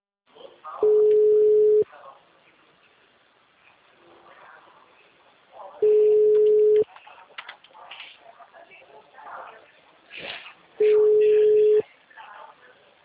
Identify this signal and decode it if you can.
Ringback tone (Europe, 425 Hz), with speech.